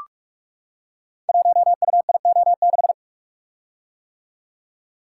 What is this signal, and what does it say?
Morse code, heard as a keyed tone, about 44 wpm, 700 Hz. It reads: 1UIO6
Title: Code Groups - 5 Letter:Numbers - 45wpm